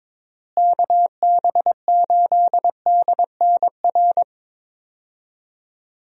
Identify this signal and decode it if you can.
Morse code, heard as a keyed tone, about 22 wpm, 700 Hz. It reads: KB8DNR
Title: USA Call Signs - 6 Characters - Repeated 2x - 22wpm